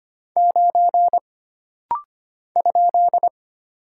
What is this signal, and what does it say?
Morse code, heard as a keyed tone, about 25 wpm, 700 Hz. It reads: 9 ?
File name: Single Character (Letter:Number:Punctuation:Prosign) - Rapid-Fire - 25wpm